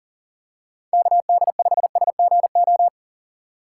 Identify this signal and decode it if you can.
Morse code, heard as a keyed tone, about 40 wpm, 700 Hz. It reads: KD5SGO
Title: USA Call Signs - 6 Characters - 4x Character Spacing - 40wpm